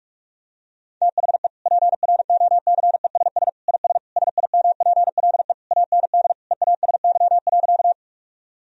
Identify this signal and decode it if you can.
Morse code, heard as a keyed tone, about 46 wpm, 700 Hz. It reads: THE PROCESS IS SIMPLE AND EASY.
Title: Sentences from Top 1000 Words - Review Part 3 - 45wpm